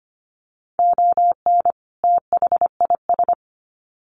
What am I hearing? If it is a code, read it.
Morse code, heard as a keyed tone, about 25 wpm, 700 Hz. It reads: ON THIS
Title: Sets of 2 Words 25wpm